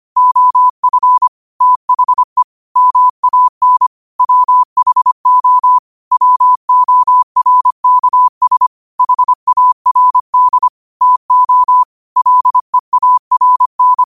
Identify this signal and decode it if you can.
Morse code, heard as a keyed tone, about 25 wpm, 1000 Hz. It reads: OF THE MAN WHO WORKS HARD TO LEARN
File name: 25wpm